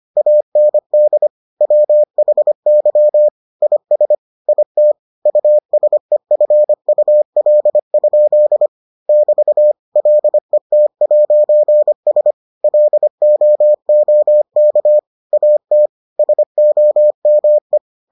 Morse code, heard as a keyed tone, about 25 wpm, 600 Hz. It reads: AND WHY IS IT USEFUL? <BT> LET'S LOOK AT SOME